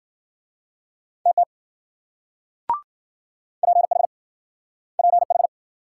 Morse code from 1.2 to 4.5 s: TT PH

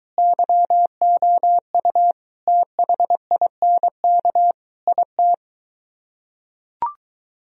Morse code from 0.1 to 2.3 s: YOU

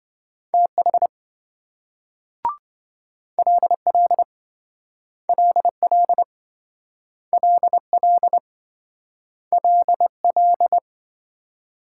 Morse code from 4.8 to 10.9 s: LL LL LL

